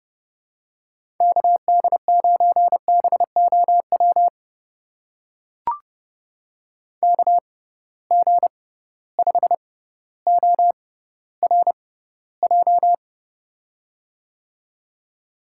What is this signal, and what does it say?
Morse code, heard as a keyed tone, about 32 wpm, spaced as for 16 wpm, 700 Hz. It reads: KD9BOW KG5ORJ
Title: USA Call Signs - 6 Characters - 6x Character Spacing - 30wpm